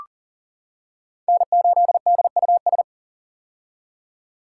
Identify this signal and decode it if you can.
Morse code, heard as a keyed tone, about 40 wpm, 700 Hz. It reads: N8DUS
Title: K1USN SST QSO Practice - 40wpm